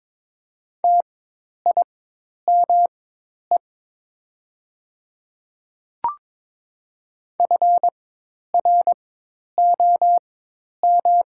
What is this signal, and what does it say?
Morse code, heard as a keyed tone, about 21 wpm, spaced as for 10 wpm, 700 Hz. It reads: TIME FROM
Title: 4x Character Spacing - Top 100 Words - 22wpm